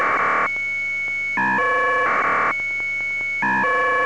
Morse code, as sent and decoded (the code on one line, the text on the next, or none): none